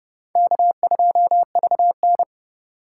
Morse code, heard as a keyed tone, about 30 wpm, 700 Hz. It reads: K2VN
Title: USA Call Signs - 4 Characters - 3x Character Spacing - 30wpm